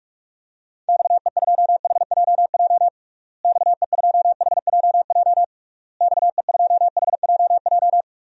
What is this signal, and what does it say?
Morse code, heard as a keyed tone, about 41 wpm, 700 Hz. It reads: XE2HJJ XE2HJJ XE2HJJ
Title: DXCC-100 Call Signs - 6 Characters - Repeated 3x - 45wpm